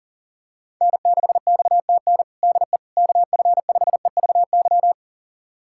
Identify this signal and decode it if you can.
Morse code, heard as a keyed tone, about 37 wpm, 700 Hz. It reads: N6XTN DE KF5EVY
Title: USA Callsign DE Callsign 40wpm